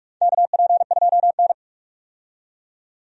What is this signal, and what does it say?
Morse code, heard as a keyed tone, about 47 wpm, 700 Hz. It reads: KP2N
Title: DXCC-100 Call Signs - 4 Characters - 45wpm